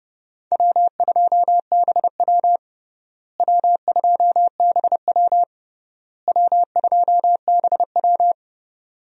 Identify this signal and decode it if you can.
Morse code, heard as a keyed tone, about 30 wpm, 700 Hz. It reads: W2BW W2BW W2BW
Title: USA Call Signs - 4 Characters - Repeated 3x - 30wpm